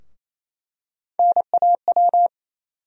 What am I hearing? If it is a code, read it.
Morse code, heard as a keyed tone, about 28 wpm, 700 Hz. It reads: NAW
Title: USA Call Sign Postfixes - 3 Letters 28wpm